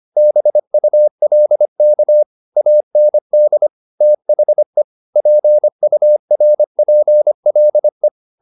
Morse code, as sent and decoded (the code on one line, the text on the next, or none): -... ..- .-.. -.- / .- -. -.. / - .... / .--. ..- .-. .--. .-.. .
BULK AND THE PURPLE